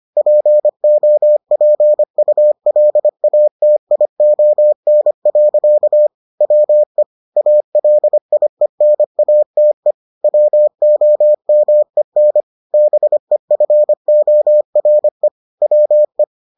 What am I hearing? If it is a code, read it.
Morse code, heard as a keyed tone, about 25 wpm, 600 Hz. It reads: POPULATION. WE ALIENATE WOMEN BEFORE WE